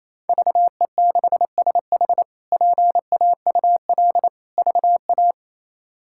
Morse code, heard as a keyed tone, about 27 wpm, 700 Hz. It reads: VE6SH PAUL VA
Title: K1USN SST QSO Practice - 28wpm